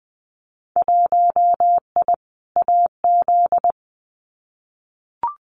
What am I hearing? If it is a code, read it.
Morse code, heard as a keyed tone, about 20 wpm, 700 Hz. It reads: 1I AZ
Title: Winter Field Day Exchanges 20wpm